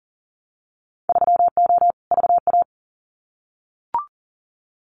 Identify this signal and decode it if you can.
Morse code, heard as a keyed tone, about 40 wpm, 700 Hz. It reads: 3O VA
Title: Winter Field Day Exchanges 40wpm